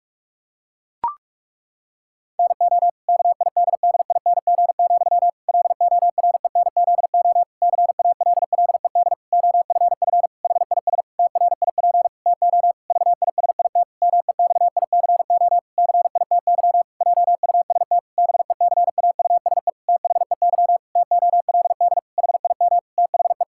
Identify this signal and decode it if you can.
Morse code, heard as a keyed tone, about 46 wpm, 700 Hz. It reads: NO KIDDING, LORENZO CALLED OFF HIS TRIP TO VISIT MEXICO CITY JUST BECAUSE THEY TOLD HIM THE
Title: Pangram Challenge v2 - 45wpm